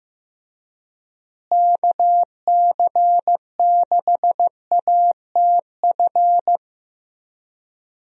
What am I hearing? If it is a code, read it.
Morse code, heard as a keyed tone, about 15 wpm, 700 Hz. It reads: KC6ATF
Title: USA Call Signs - 6 Characters - Repeated 3x - 15wpm